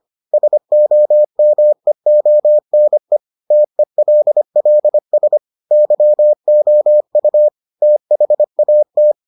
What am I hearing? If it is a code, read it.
Morse code, heard as a keyed tone, about 25 wpm, 600 Hz. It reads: SOMEONE TELLS YOU THAT